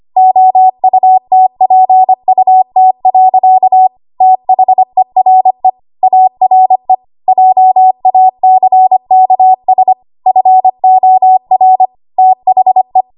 Morse code, heard as a keyed tone, about 25 wpm, 750 Hz. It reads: OUTPUT. THERE ARE JACKS FOR THE
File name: Archive / 25 WPM / 25WPM